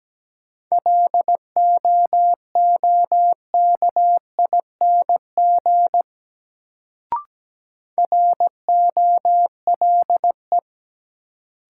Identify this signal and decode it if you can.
Morse code, heard as a keyed tone, about 17 wpm, 700 Hz. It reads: LOOKING ROLE
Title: Sentences from Top 1000 Words - Review Part 6 - New Word Drill - 17wpm